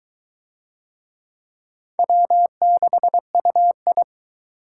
Morse code, heard as a keyed tone, about 22 wpm, 700 Hz. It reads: W6UI